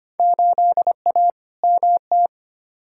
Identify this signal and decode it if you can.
Morse code, heard as a keyed tone, about 25 wpm, 700 Hz. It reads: 8A MT
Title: Field Day Exchanges Encore - 25wpm